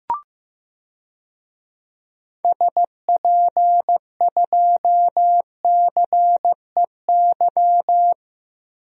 Morse code, heard as a keyed tone, about 16 wpm, 700 Hz. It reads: SP2CEY
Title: DXCC-100 Call Signs - 6 Characters - Repeated 3x - 15wpm